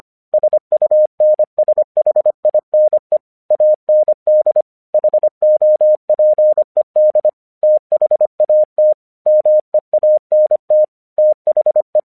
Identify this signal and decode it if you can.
Morse code, heard as a keyed tone, about 25 wpm, 610 Hz. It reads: SUNSHINE AND HOPED THAT MEANT THE
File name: English 25wpm